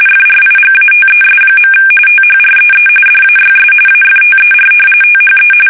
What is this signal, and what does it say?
RTTY, about 45.45 baud. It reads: FRIGATES. THREE DAYS AFTER THE
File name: fsk850 / usnavy2